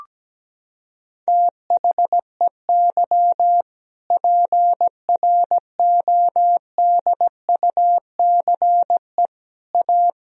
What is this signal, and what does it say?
Morse code, heard as a keyed tone, about 17 wpm, 700 Hz. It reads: THEY PRODUCE A
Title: Sentences from Top 1000 Words - Review Part 3 - 17wpm